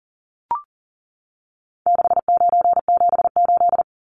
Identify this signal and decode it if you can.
Morse code, heard as a keyed tone, about 40 wpm, 700 Hz. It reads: 6978
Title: Numbers - 4-Digit 40wpm